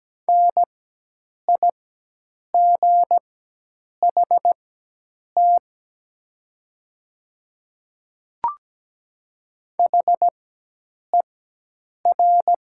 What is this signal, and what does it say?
Morse code, heard as a keyed tone, about 17 wpm, spaced as for 8 wpm, 700 Hz. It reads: NIGHT HER
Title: 4x Character Spacing - Top 300 Words - 17wpm